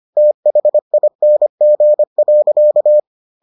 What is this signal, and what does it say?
Morse code, heard as a keyed tone, about 25 wpm, 600 Hz. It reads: THING.